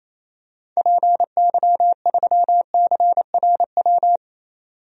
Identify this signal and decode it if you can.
Morse code, heard as a keyed tone, about 28 wpm, 700 Hz. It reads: PY3CRW
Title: DXCC-100 Call Signs - 6 Characters - 28wpm